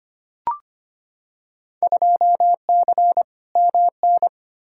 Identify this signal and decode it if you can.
Morse code, heard as a keyed tone, about 25 wpm, 700 Hz. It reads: 2C MN
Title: Field Day Exchanges Encore - 25wpm